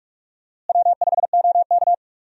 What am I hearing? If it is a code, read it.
Morse code, heard as a keyed tone, about 42 wpm, 700 Hz. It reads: W5OK
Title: USA Call Signs - 4 Characters - Repeated 2x - 45wpm